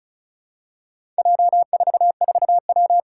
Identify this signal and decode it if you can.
Morse code, heard as a keyed tone, about 36 wpm, 700 Hz. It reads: J44W
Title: Four Challenging Letter-Number Combinations 35wpm